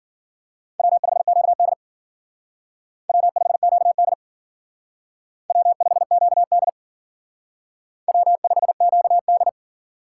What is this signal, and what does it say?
Morse code, heard as a keyed tone, about 52 wpm, 700 Hz. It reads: W5QD W5QD W5QD W5QD